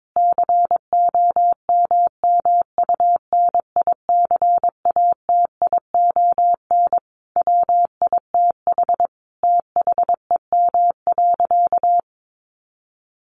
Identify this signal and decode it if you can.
Morse code, heard as a keyed tone, about 22 wpm, 700 Hz. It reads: COMMUNICATION WITH THEM.
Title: Sentences from Top 1400 Words 22wpm